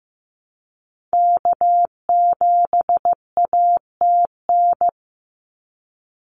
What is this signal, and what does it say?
Morse code, heard as a keyed tone, about 15 wpm, 700 Hz. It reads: K7ATN
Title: NA SOTA Activator Callsigns 15wpm